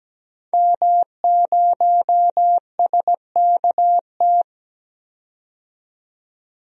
Morse code, heard as a keyed tone, about 17 wpm, 700 Hz. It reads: M0SKT
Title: DXCC-100 Call Signs - 5 Characters - 17wpm